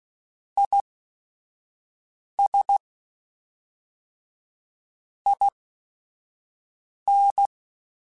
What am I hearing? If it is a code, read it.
Morse code, heard as a keyed tone, about 16 wpm, spaced as for 5 wpm, 800 Hz. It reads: IS IN
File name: Technical Subjects / Technical Subjects-mp3s / XC @5